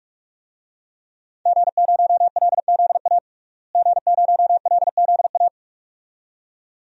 Morse code, heard as a keyed tone, about 41 wpm, 700 Hz. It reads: G0LZA G0LZA
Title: DXCC-100 Call Signs - 5 Characters - Repeated 2x - 45wpm